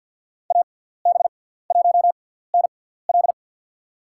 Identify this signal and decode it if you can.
Morse code, heard as a keyed tone, about 51 wpm, spaced as for 17 wpm, 690 Hz. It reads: AB1NL
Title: USA Call Signs - 5 Characters - 6x Character Spacing - 50wpm